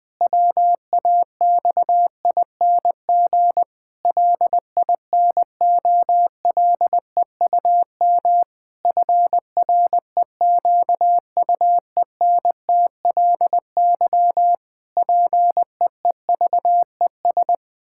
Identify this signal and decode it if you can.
Morse code, heard as a keyed tone, about 19 wpm, 700 Hz. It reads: WAXING LINOLEUM FREQUENTLY PEEVES
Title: Pangram Challenge v2 - 20wpm